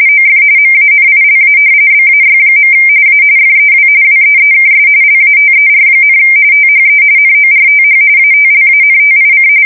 RTTY, about 45.45 baud. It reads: OF MANILA BAY. A FEW WEEKS LATER, THE NORTH ATLANTIC
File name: fsk170 / usnavy3